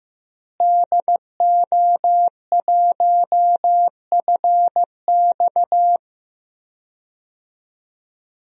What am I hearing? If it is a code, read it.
Morse code, heard as a keyed tone, about 15 wpm, 700 Hz. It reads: DO1FX
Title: DXCC-100 Call Signs - 5 Characters - Repeated 3x - 15wpm